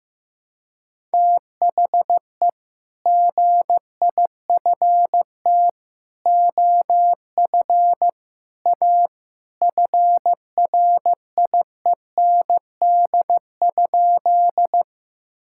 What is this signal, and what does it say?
Morse code, heard as a keyed tone, about 15 wpm, 700 Hz. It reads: THE GIFT OF A FRIEND?
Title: Sentences from Top 1000 Words - Review Part 6 - 15wpm